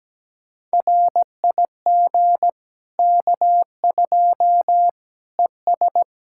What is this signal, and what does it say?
Morse code, heard as a keyed tone, about 17 wpm, 700 Hz. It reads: RIG K2 ES
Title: Practice QSO - Exchange 2 - Info - 17wpm